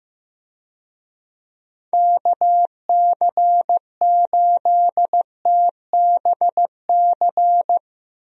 Morse code, heard as a keyed tone, about 15 wpm, 700 Hz. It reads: KC8TBC
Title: USA Call Signs - 6 Characters - Repeated 2x - 15wpm